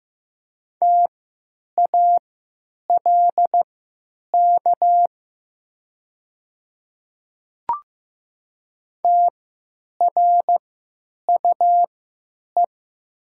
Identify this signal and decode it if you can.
Morse code, heard as a keyed tone, about 15 wpm, spaced as for 9 wpm, 700 Hz. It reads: TALK TRUE
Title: 3x Character Spacing - Top 500 Words - 15wpm